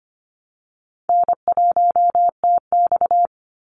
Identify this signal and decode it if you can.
Morse code, heard as a keyed tone, about 25 wpm, 700 Hz. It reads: N1TX
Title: Easy Call Signs Repeated 3x - 25wpm